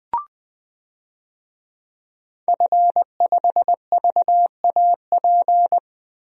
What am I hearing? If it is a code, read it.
Morse code, heard as a keyed tone, about 20 wpm, 700 Hz. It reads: F5VAP
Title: DXCC-100 Call Signs - 5 Characters - 20wpm